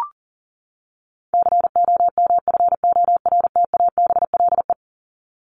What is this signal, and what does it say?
Morse code, heard as a keyed tone, about 40 wpm, 700 Hz. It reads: COMFORTABLE
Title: Top 751-1000 Words 40wpm